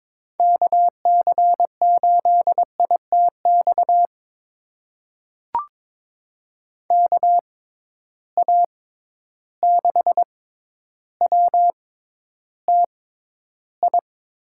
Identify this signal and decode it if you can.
Morse code, heard as a keyed tone, about 22 wpm, spaced as for 11 wpm, 700 Hz. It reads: KC8ITX KA6WTI
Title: USA Call Signs - 6x Character Spacing - 22wpm